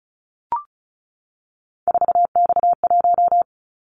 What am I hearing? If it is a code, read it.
Morse code, heard as a keyed tone, about 36 wpm, 690 Hz. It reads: 4X1
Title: Callsign Prefixes for Top 50 DXCC - 35wpm